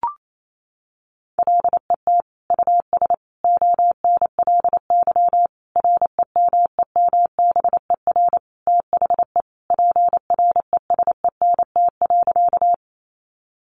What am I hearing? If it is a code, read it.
Morse code, heard as a keyed tone, about 28 wpm, 700 Hz. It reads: LET US ONLY REMEMBER THE PRESENT.
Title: Sentences from Top 800 Words 28wpm